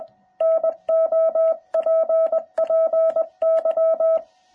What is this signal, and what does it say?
Morse code, heard as a keyed tone, about 20 wpm, 660 Hz. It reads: NOPPY